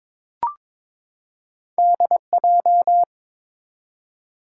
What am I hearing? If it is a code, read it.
Morse code, heard as a keyed tone, about 22 wpm, 700 Hz. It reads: DJ
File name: practice / 22wpm / Two Letter-Number Combinations - 22wpm